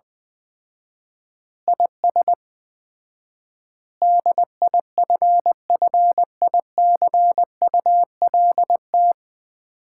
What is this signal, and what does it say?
Morse code, heard as a keyed tone, about 20 wpm, 700 Hz. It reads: IS DIFFICULT
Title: 4x Word-Spacing - Sets of 4 Words - 20wpm